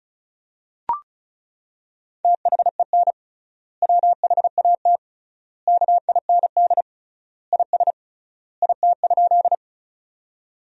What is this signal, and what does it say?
Morse code, heard as a keyed tone, about 34 wpm, 700 Hz. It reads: THEN WHAT KIND IS IT?
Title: 3x Word-Spacing - Sentences from Top 200 Words - 35wpm